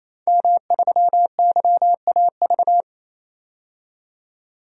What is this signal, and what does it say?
Morse code, heard as a keyed tone, about 28 wpm, 700 Hz. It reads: M3YAV
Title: DXCC-100 Call Signs - 5 Characters - 28wpm